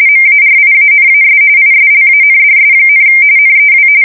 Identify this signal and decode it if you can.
RTTY, about 45.45 baud. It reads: MARIANAS ISLANDS, NEW
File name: fsk170 / USS-New-Jersey